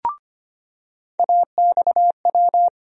Morse code, heard as a keyed tone, about 25 wpm, 700 Hz. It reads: AXW
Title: USA Call Sign Postfixes - 3 Letters 25wpm